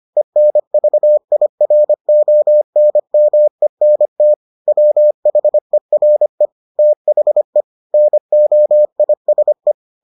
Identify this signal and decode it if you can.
Morse code, heard as a keyed tone, about 25 wpm, 600 Hz. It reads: ENVIRONMENT WHERE THE NOISE